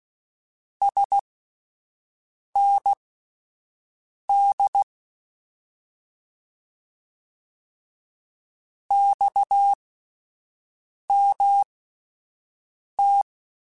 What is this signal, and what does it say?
Morse code, heard as a keyed tone, about 16 wpm, spaced as for 5 wpm, 800 Hz. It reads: SND XMT
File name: Abreviations / Abreviations mp3s / Abreviations@5